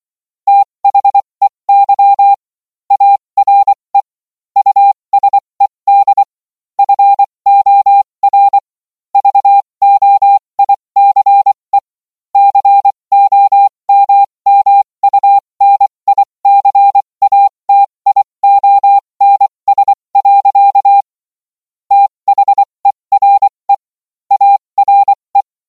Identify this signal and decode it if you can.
Morse code, heard as a keyed tone, about 24 wpm, spaced as for 20 wpm, 800 Hz. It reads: THEY ARE USED FOR VOICE COMMUNICATIONS. THERE ARE